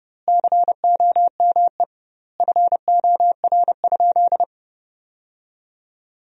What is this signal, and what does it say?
Morse code, heard as a keyed tone, about 32 wpm, 700 Hz. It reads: COME FOR?
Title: 2x Word-Spacing - Sentences from Top 100 Words - 30wpm